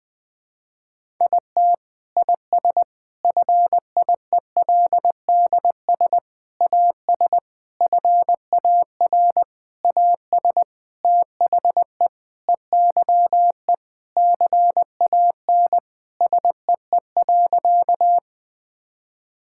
Morse code, heard as a keyed tone, about 20 wpm, 700 Hz. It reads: IT IS FIELDS AS FAR AS THE EYE CAN SEE.
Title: Sentences with 7-Word Phrases - Spoken Beginning - 20wpm